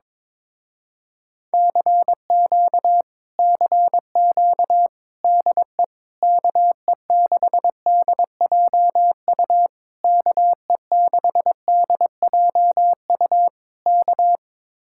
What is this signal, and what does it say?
Morse code, heard as a keyed tone, about 22 wpm, 700 Hz. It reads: CQ CQ DE KE6DJU KE6DJU K
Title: Speed-Racing Common QSO Phrases Encore - 22wpm